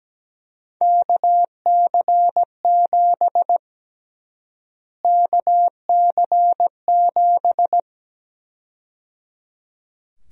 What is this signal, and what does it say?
Morse code, heard as a keyed tone, about 17 wpm, 700 Hz. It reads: KC7 KC7
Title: USA Call Sign Prefixes - Repeated 2x - 17wpm